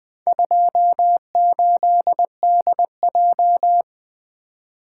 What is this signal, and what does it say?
Morse code, heard as a keyed tone, about 20 wpm, 700 Hz. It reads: 28DJ
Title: Four Letter-Number Combinations - 20wpm